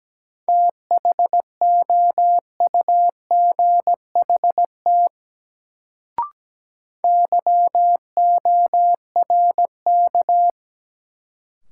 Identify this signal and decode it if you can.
Morse code, heard as a keyed tone, about 17 wpm, 700 Hz. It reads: THOUGHT YORK
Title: Sets of 2 Words - New Word Drill - 17wpm